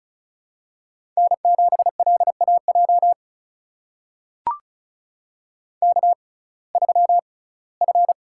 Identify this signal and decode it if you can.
Morse code, heard as a keyed tone, about 34 wpm, spaced as for 21 wpm, 700 Hz. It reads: N7LAJ K3F
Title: USA Call Signs - 5 Characters - 6x Character Spacing - 35wpm